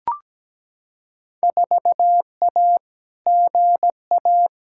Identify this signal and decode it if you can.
Morse code, heard as a keyed tone, about 17 wpm, 700 Hz. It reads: 4A GA
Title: Field Day Exchanges - 17wpm